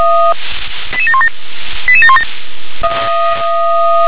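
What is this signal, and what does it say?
Five-tone selective call (ZVEI), heard as quick runs of five tones, tones of about 70 ms. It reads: 88717 88717
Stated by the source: ZVEI five-tone call received on an SDR.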